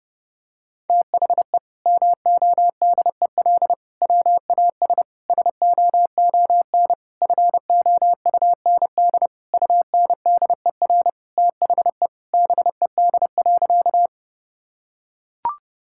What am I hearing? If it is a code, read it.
Morse code, heard as a keyed tone, about 30 wpm, 700 Hz. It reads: THE MODEL WAS SOON FOUND UNDER THE BED.
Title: Sentences from Top 1000 Words - Review Part 5 - 30wpm